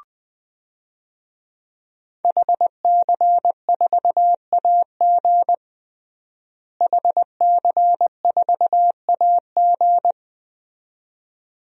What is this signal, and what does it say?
Morse code, heard as a keyed tone, about 21 wpm, 700 Hz. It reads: HC4AG HC4AG
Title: DXCC-100 Call Signs - 5 Characters - Repeated 2x - 20wpm